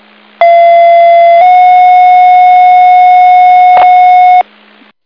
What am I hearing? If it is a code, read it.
Two-tone page, heard as a one-second tone, then a three-second tone, about 680 then 730 Hz, with noise. It is heard from 0.4 to 4.4 s.